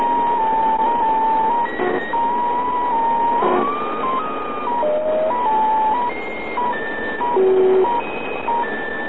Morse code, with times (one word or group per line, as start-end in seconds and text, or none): none